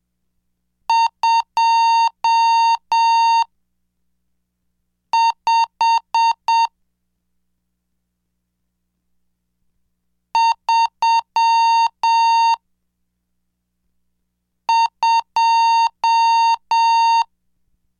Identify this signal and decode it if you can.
Morse code, heard as a keyed tone, about 7 wpm, spaced as for 3 wpm, 920 Hz. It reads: 25 32
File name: cursos / CDestudo / 24 Track 24